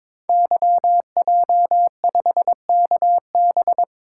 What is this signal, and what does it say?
Morse code, heard as a keyed tone, about 22 wpm, 700 Hz. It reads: YJ5KB
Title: Code Groups - 5 Letter:Numbers - 22wpm